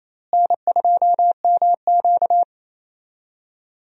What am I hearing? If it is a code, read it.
Morse code, heard as a keyed tone, about 28 wpm, 700 Hz. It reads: N2MQ
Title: USA Call Signs - 4 Characters - Repeated 3x - 28wpm